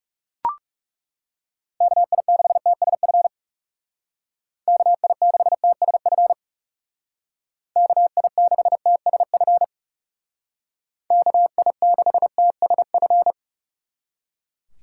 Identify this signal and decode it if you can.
Morse code, heard as a keyed tone, about 40 wpm, 700 Hz. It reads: KI6TSF KI6TSF KI6TSF KI6TSF